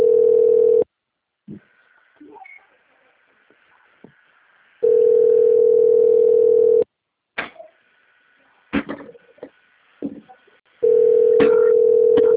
Ringback tone (North America), 440 + 480 Hz.
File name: out- - - - - .30927